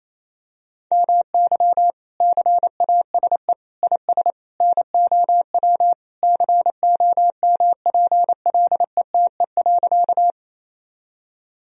Morse code, heard as a keyed tone, about 28 wpm, 700 Hz. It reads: MY CASE IS NOW COMPLETE.
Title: Sentences from Top 1000 Words - Review Part 2 - 28wpm